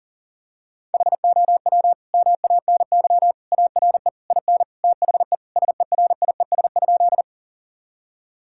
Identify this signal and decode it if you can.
Morse code, heard as a keyed tone, about 40 wpm, 700 Hz. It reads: HOW MANY ARE IN THE SERIES?
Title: Sentences from Top 1000 Words - Review Part 5 - 40wpm